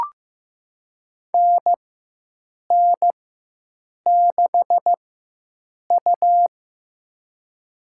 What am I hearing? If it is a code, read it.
Morse code, heard as a keyed tone, about 15 wpm, spaced as for 7 wpm, 700 Hz. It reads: NN6U
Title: USA Call Signs - 4 Characters - 4x Character Spacing - 15wpm